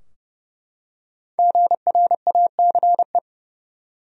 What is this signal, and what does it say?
Morse code, heard as a keyed tone, about 29 wpm, 700 Hz. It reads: GRACE
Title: USA Common Names - 30wpm